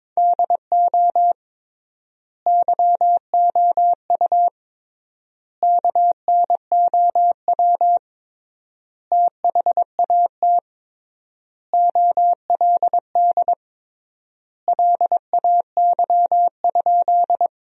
Morse code, heard as a keyed tone, about 22 wpm, 700 Hz. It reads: DO YOU KNOW THAT OLD LAY?
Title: 3x Word-Spacing - Sentences from Top 400 Words - 22wpm